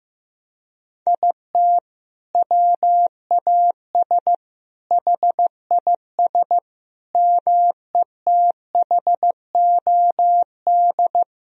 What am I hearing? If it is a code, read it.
Morse code, heard as a keyed tone, about 15 wpm, 700 Hz. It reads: IT WAS HIS METHOD
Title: Sentences from Top 1000 Words - Review Part 3 - 15wpm